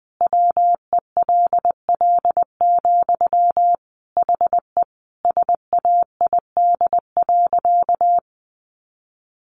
Morse code, heard as a keyed tone, about 20 wpm, 700 Hz. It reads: WELL, HE SAID.
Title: Sentences from Top 500 Words 20wpm